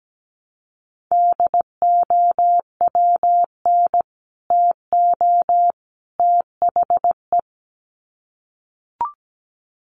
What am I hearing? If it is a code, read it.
Morse code, heard as a keyed tone, about 17 wpm, 700 Hz. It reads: DOWN TO THE
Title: Sets of 3 Words 17wpm